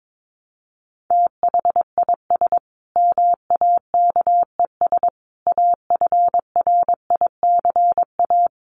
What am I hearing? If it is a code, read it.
Morse code, heard as a keyed tone, about 22 wpm, 700 Hz. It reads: THIS MAKES AFRICA
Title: A Sound Of Thunder 22wpm